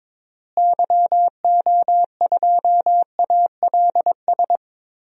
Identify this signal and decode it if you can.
Morse code, heard as a keyed tone, about 22 wpm, 700 Hz. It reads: YO2ALS
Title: DXCC-100 Call Signs - 6 Characters - Repeated 3x - 22wpm